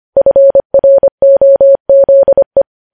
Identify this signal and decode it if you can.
Morse code, heard as a keyed tone, about 25 wpm, 550 Hz. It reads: FROZE
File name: podcast / News headlines-1x-25x25